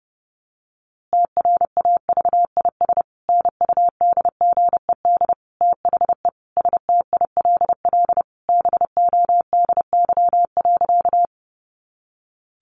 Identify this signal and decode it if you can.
Morse code, heard as a keyed tone, about 30 wpm, 700 Hz. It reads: TRAVIS NUDGED THE STILL BODY.
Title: A Sound Of Thunder 30wpm